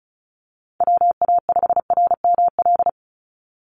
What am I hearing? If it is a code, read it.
Morse code, heard as a keyed tone, about 34 wpm, 700 Hz. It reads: WA5RML
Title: Call Signs - 35wpm